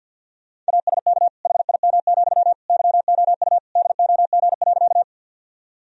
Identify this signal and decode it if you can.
Morse code, heard as a keyed tone, about 50 wpm, 700 Hz. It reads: ASK HIM, YOU DOG.
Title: Sentences from Top 1000 Words - Review Part 2 - 50wpm